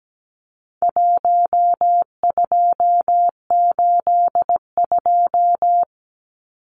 Morse code, heard as a keyed tone, about 17 wpm, 700 Hz. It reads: 1282
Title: Numbers - 4-Digit 17wpm